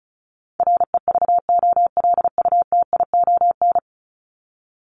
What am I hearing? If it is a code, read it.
Morse code, heard as a keyed tone, about 34 wpm, 700 Hz. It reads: REVOLUTION
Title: Top 1751-2000 Words 35wpm